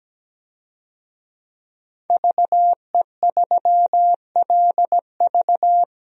Morse code, heard as a keyed tone, about 17 wpm, 700 Hz. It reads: VE3LV
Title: DXCC-100 Call Signs - 5 Characters - 17wpm